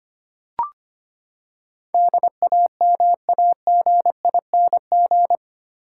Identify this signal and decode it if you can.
Morse code, heard as a keyed tone, about 25 wpm, 700 Hz. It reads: DAMAGING
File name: Eight Letter Words - 25wpm